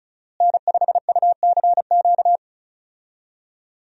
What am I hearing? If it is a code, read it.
Morse code, heard as a keyed tone, about 36 wpm, 700 Hz. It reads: N5UCQ
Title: USA Call Signs - 5 Characters - Repeated 3x - 35wpm